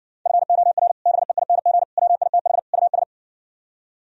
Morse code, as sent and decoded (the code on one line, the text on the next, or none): ..-. --- .-. / -... .. -. --. / .-- .. - .... / ..- ...
FOR BEING WITH US